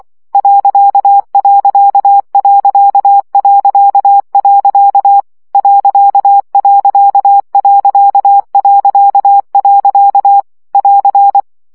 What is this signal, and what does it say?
Morse code, heard as a keyed tone, about 24 wpm, 800 Hz. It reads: ..... ..... <AR>
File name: CW gyakorlas pont 20 baud